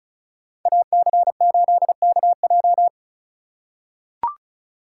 Morse code, heard as a keyed tone, about 34 wpm, 700 Hz. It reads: AC8KJ
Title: USA Call Signs - 5 Characters - 2x Character Spacing - 35wpm